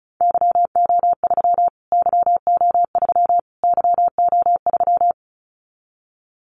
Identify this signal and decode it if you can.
Morse code, heard as a keyed tone, about 34 wpm, 700 Hz. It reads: YO3 YO3 YO3
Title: Callsign Prefixes for Top 25 DXCC - Repeated 3x - 35wpm